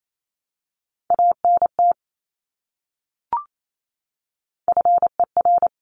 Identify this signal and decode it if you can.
Morse code, heard as a keyed tone, about 28 wpm, 700 Hz. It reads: ANT FER
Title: Abbreviations Repeated 3x - 28wpm